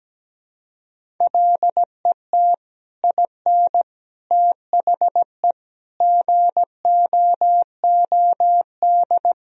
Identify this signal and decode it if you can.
Morse code, heard as a keyed tone, about 17 wpm, 700 Hz. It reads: LET IN THE GOOD
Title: Sentences from Top 1000 Words - Review Part 1 - 17wpm